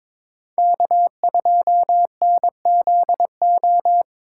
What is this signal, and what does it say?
Morse code, heard as a keyed tone, about 26 wpm, 700 Hz. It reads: K2NZO
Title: Speed Racing - USA Call Signs 22wpm